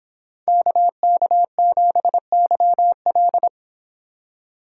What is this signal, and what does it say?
Morse code, heard as a keyed tone, about 25 wpm, 700 Hz. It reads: KK7YL